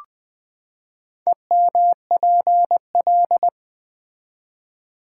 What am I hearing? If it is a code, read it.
Morse code, heard as a keyed tone, about 20 wpm, 700 Hz. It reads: EMPL
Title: Build A Word - 4 Letters At A Time - 10-Letter Words - 20wpm